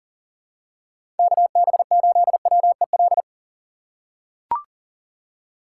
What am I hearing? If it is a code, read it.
Morse code, heard as a keyed tone, about 40 wpm, 700 Hz. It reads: KB8WEL
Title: USA Call Signs - 6 Characters - 2x Character Spacing - 40wpm